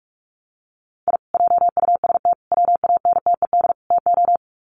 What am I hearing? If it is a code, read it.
Morse code, heard as a keyed tone, about 44 wpm, 700 Hz. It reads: I JUST WANTED TO
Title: Sets of 4 Words 45wpm